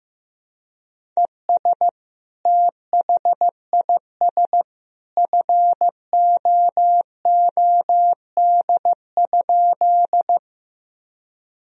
Morse code, heard as a keyed tone, about 15 wpm, 700 Hz. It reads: ES THIS FOOD?
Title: Sentences from Top 300 Words and Abbreviations 15wpm